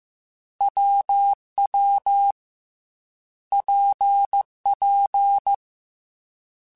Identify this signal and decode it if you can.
Morse code, heard as a keyed tone, about 15 wpm, 800 Hz. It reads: WW PP
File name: Difficult Combos / Difficult Combos-mp3s / WPJ WPJ1 @13